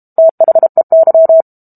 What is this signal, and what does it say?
Morse code, heard as a keyed tone, about 32 wpm, 650 Hz. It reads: THEY
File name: Book Two / 06-Chapter 6 - 32 WPM